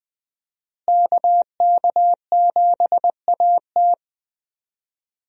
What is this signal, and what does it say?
Morse code, heard as a keyed tone, about 20 wpm, 700 Hz. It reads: KK7AT
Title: USA Call Signs - 5 Characters - Repeated 3x - 20wpm